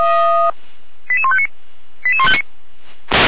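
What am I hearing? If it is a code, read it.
Five-tone selective call (ZVEI), heard as quick runs of five tones, tones of about 80 ms. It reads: 88159 88159
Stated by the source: ZVEI five-tone selective call, code 88150, received with an SDR.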